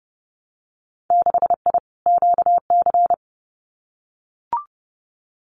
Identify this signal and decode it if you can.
Morse code, heard as a keyed tone, about 30 wpm, 700 Hz. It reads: 6I QC
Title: Winter Field Day Exchanges Encore 30wpm